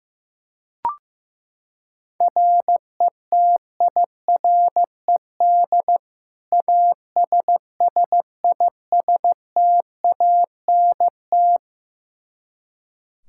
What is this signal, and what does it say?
Morse code, heard as a keyed tone, about 15 wpm, 700 Hz. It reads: RETIRED ASSISTANT
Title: K1USN SST QSO Practice 15wpm